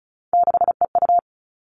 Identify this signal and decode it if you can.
Morse code, heard as a keyed tone, about 34 wpm, 700 Hz. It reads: 6EU
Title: All Cut Numbers - 3-Digit 35wpm